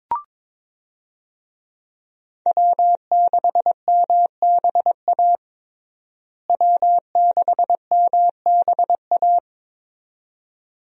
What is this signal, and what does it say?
Morse code, heard as a keyed tone, about 21 wpm, 700 Hz. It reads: W6MBA W6MBA
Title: USA Call Signs - 5 Characters - Repeated 2x - 22wpm